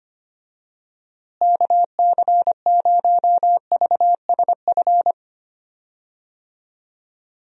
Morse code, heard as a keyed tone, about 25 wpm, 700 Hz. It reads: KC0VSF
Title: USA Call Signs - 6 Characters - Repeated 2x - 25wpm